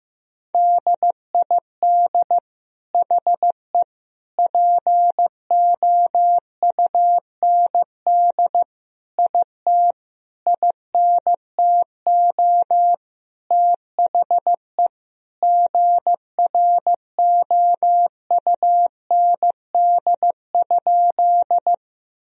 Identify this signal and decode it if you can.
Morse code, heard as a keyed tone, about 15 wpm, 700 Hz. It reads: DID HE POUND IT INTO THE GROUND?
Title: Sentences from Top 1000 Words - Review Part 2 - 15wpm